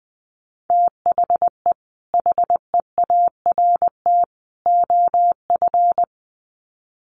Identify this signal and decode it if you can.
Morse code, heard as a keyed tone, about 20 wpm, 700 Hz. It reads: THE HEART OF
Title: Sets of 3 Words 20wpm